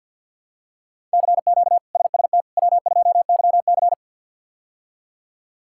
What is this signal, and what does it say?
Morse code, heard as a keyed tone, about 52 wpm, 700 Hz. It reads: CQ SST W2YC
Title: K1USN SST QSO Practice - 50wpm